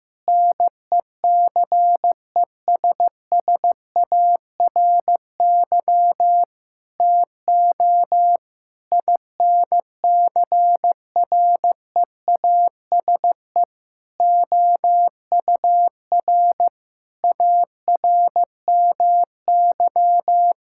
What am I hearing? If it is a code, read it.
Morse code, heard as a keyed tone, about 16 wpm, 700 Hz. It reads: NECESSARY TO INCREASE OUR ARMY
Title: Sentences from Top 1000 Words - Review Part 6 - 15wpm